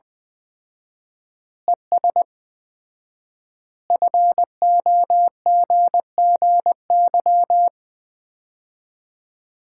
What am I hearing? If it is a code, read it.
Morse code, heard as a keyed tone, about 20 wpm, 700 Hz. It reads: ES FOGGY
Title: Common QSO Phrases - 4x Word-Spacing - 20wpm